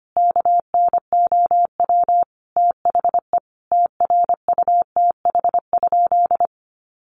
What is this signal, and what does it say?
Morse code, heard as a keyed tone, about 25 wpm, 700 Hz. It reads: KNOW THE TRUTH?
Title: Sentences from Top 800 Words 25wpm